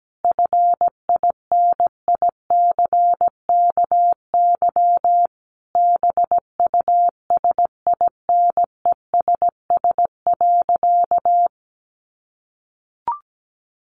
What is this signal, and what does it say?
Morse code, heard as a keyed tone, about 17 wpm, 700 Hz. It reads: FINICKY BUSINESS.
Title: A Sound Of Thunder 17wpm